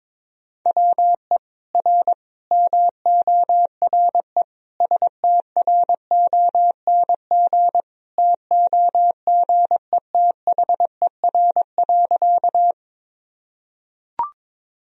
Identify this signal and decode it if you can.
Morse code, heard as a keyed tone, about 21 wpm, 700 Hz. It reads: WE R MORE STRONG TOGETHER.
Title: Sentences from Top 500 Words and Abbreviations 22wpm